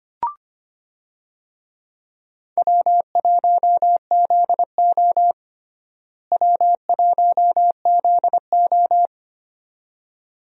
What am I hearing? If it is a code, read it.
Morse code, heard as a keyed tone, about 25 wpm, 700 Hz. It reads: W1ZO W1ZO
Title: USA Call Signs - 4 Characters - Repeated 2x - 25wpm